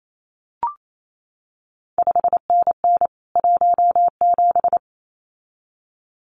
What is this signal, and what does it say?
Morse code, heard as a keyed tone, about 28 wpm, 700 Hz. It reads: 5NN 17
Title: CQ WW DX Contest Exchange Encore - 28wpm